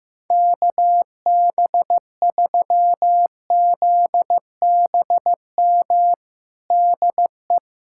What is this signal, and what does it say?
Morse code, heard as a keyed tone, about 15 wpm, 700 Hz. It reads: KB3ZBM DE
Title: USA Callsign DE Callsign - 6 Characters - 15wpm